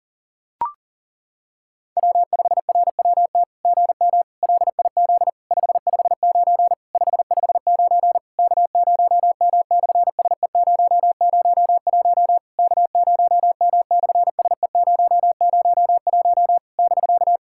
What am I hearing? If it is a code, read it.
Morse code, heard as a keyed tone, about 40 wpm, 700 Hz. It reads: W5RWT GM LIZ 559 559 K0M/SE001 K0M/SE001 <BK>
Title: SOTA QSO Practice 40wpm